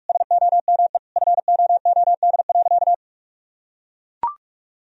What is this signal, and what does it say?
Morse code, heard as a keyed tone, about 42 wpm, 700 Hz. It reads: SOME FOOD.
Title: Sentences from Top 1000 Words - Review Part 4 - 45wpm